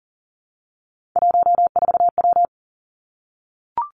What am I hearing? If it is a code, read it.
Morse code, heard as a keyed tone, about 40 wpm, 700 Hz. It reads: 14W
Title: Three Challenging Letter-Number Combinations 40wpm